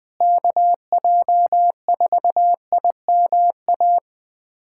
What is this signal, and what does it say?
Morse code, heard as a keyed tone, about 21 wpm, 700 Hz. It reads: KJ4IMA
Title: USA Callsign DE Callsign - 6 Characters - 20wpm